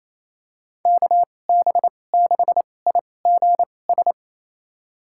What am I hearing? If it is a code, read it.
Morse code, heard as a keyed tone, about 28 wpm, spaced as for 20 wpm, 690 Hz. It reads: KB6IGS
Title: USA Call Signs - 2x Character Spacing - 28wpm